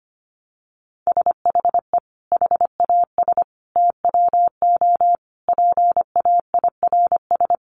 Morse code, heard as a keyed tone, about 25 wpm, 700 Hz. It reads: SHE HAS TWO PAIRS
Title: Additional Easy Phrases 25wpm